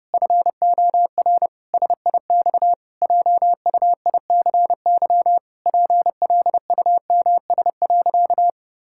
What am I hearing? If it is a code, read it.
Morse code, heard as a keyed tone, about 30 wpm, 700 Hz. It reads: FOR SIX JUICY PLUMS.
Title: Pangram Challenge v2 - 30wpm